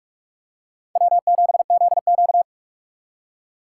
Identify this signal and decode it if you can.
Morse code, heard as a keyed tone, about 41 wpm, 700 Hz. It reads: W7ZQ
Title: USA Call Signs - 2x Character Spacing - 45wpm